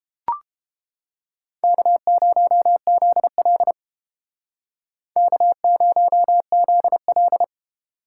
Morse code, heard as a keyed tone, about 31 wpm, 700 Hz. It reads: K0ZL K0ZL